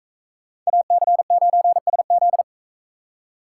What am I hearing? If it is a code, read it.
Morse code, heard as a keyed tone, about 44 wpm, 700 Hz. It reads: AC9SZ